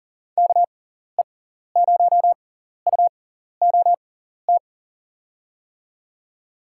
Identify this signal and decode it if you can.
Morse code, heard as a keyed tone, about 40 wpm, spaced as for 14 wpm, 700 Hz. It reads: KE0UOT
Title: USA Call Signs - 6 Characters - 6x Character Spacing - 40wpm